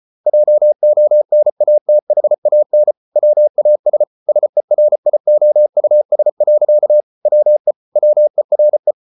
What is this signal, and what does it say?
Morse code, heard as a keyed tone, about 34 wpm, 600 Hz. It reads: JONATHAN WAS SERIOUS. WE WERE